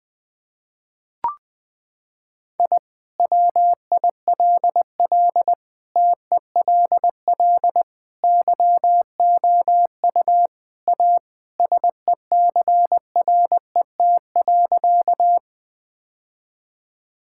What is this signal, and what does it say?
Morse code, heard as a keyed tone, about 21 wpm, 700 Hz. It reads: I WILL TELL YOU A SECRET.
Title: Sentences with 2-Word Phrases - 20wpm